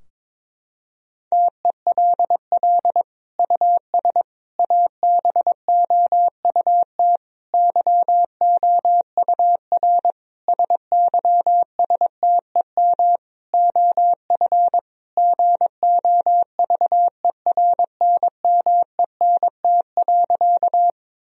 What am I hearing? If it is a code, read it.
Morse code, heard as a keyed tone, about 22 wpm, 700 Hz. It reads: TELL US ABOUT YOUR SYSTEM OF GOVERNMENT.
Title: Sentences from Top 1000 Words - Review Part 3 - 22wpm